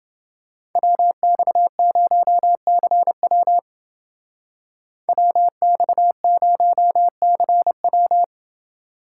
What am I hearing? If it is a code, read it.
Morse code, heard as a keyed tone, about 28 wpm, 700 Hz. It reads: WX0CW WX0CW